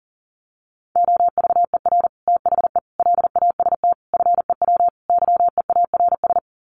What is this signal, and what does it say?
Morse code, heard as a keyed tone, about 40 wpm, 700 Hz. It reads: OVER THE LAST FEW YEARS
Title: Sets of 5 Words Encore 40wpm